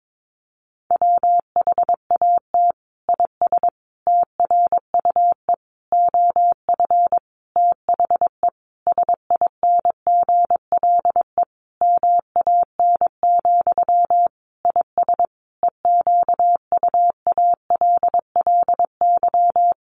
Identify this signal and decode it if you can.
Morse code, heard as a keyed tone, about 22 wpm, 700 Hz. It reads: WHAT IS TRUE OF THE SINGLE MAN, IS EQUALLY
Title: Sentences from Top 1400 Words 22wpm